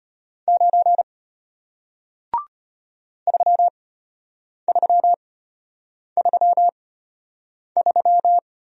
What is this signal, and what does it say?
Morse code, heard as a keyed tone, about 34 wpm, spaced as for 8 wpm, 700 Hz. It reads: 9 3333